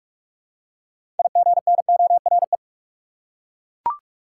Morse code, heard as a keyed tone, about 42 wpm, 700 Hz. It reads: IGNORE
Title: Six Letter Words - 45wpm